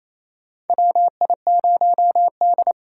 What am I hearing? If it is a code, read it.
Morse code, heard as a keyed tone, about 28 wpm, 700 Hz. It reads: WI0D
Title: USA Call Signs - 4 Characters - 3x Character Spacing - 28wpm